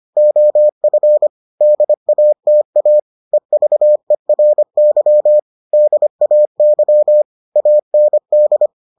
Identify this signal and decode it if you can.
Morse code, heard as a keyed tone, about 25 wpm, 600 Hz. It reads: OF DATA EVERY DAY AND